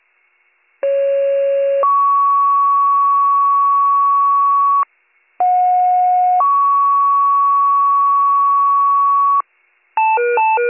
Two-tone page, heard as a one-second tone, then a three-second tone, about 550 then 1120 Hz, about 730 then 1120 Hz. It is heard from 0.8 to 9.4 s.